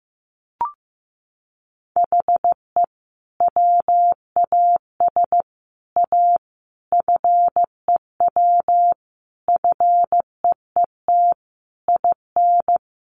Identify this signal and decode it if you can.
Morse code, heard as a keyed tone, about 15 wpm, 700 Hz. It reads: HE WAS A FEW FEET IN
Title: Sentences from Top 1400 Words 15wpm